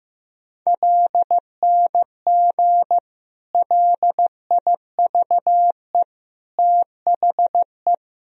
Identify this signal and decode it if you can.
Morse code, heard as a keyed tone, about 15 wpm, 700 Hz. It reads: LNG LIVE THE
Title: Sentences from Top 400 Words and Abbreviations 15wpm